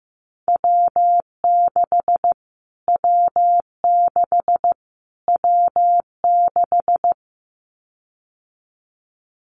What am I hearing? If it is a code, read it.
Morse code, heard as a keyed tone, about 15 wpm, 700 Hz. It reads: W6 W6 W6
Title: Callsign Prefixes for Top 25 DXCC - Repeated 3x - 15wpm